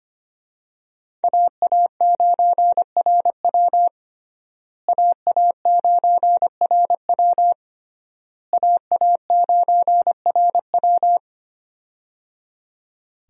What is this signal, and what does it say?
Morse code, heard as a keyed tone, about 25 wpm, 700 Hz. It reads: AA9RW AA9RW AA9RW
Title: USA Call Signs - 5 Characters - Repeated 3x - 25wpm